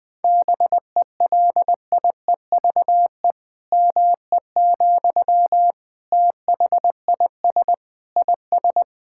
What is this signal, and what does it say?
Morse code, heard as a keyed tone, about 21 wpm, 700 Hz. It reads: BELIEVE ME, THIS IS
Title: Sentences from Top 1000 Words - Review Part 6 - 20wpm